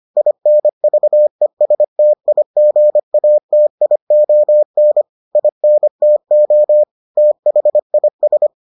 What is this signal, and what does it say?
Morse code, heard as a keyed tone, about 25 wpm, 600 Hz. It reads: INVESTIGATION INTO THIS